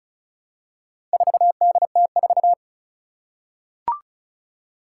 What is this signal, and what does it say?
Morse code, heard as a keyed tone, about 36 wpm, 700 Hz. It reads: 4DT4
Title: Four Challenging Letter-Number Combinations 35wpm